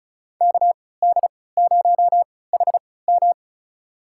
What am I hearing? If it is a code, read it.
Morse code, heard as a keyed tone, about 36 wpm, spaced as for 20 wpm, 690 Hz. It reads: KD0HM
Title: USA Call Signs - 5 Characters - 3x Character Spacing - 35wpm